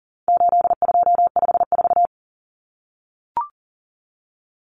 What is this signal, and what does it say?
Morse code, heard as a keyed tone, about 40 wpm, 700 Hz. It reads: 8254
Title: Common Cut Numbers - 4-Digit 40wpm